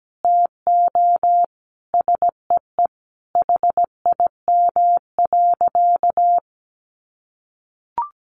Morse code, heard as a keyed tone, about 17 wpm, 700 Hz. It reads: TO SEE HIM.
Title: Sentences from Top 1500 Words 17wpm